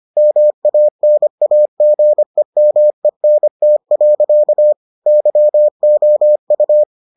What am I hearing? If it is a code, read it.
Morse code, heard as a keyed tone, about 25 wpm, 600 Hz. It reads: MANAGEMENT. YOU